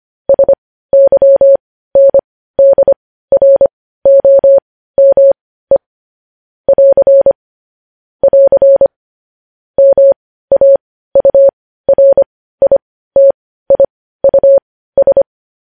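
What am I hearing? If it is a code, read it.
Morse code, heard as a keyed tone, about 25 wpm, spaced as for 15 wpm, 550 Hz. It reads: SYNDROME <AR> <AR> MAURITIUS
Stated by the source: Morse practice sending news headlines at 15 wpm.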